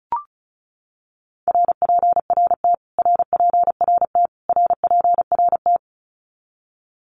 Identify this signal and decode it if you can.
Morse code, heard as a keyed tone, about 34 wpm, 700 Hz. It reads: RPRT RPRT RPRT
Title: Abbreviations Repeated 3x - 35wpm